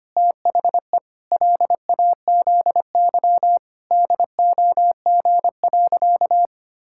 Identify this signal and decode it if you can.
Morse code, heard as a keyed tone, about 25 wpm, 700 Hz. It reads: THE LAZY DOG.
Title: Pangram Challenge v2 - 25wpm